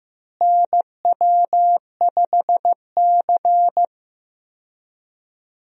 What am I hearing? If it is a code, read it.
Morse code, heard as a keyed tone, about 15 wpm, 700 Hz. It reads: NW5C
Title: USA Call Signs - 4 Characters - Repeated 3x - 15wpm